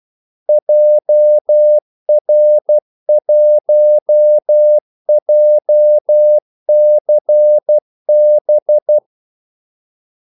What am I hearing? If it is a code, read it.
Morse code, heard as a keyed tone, about 12 wpm, 600 Hz. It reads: JR1JCB
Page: 2 ekstra ordmellemrum mellem kaldesignalerne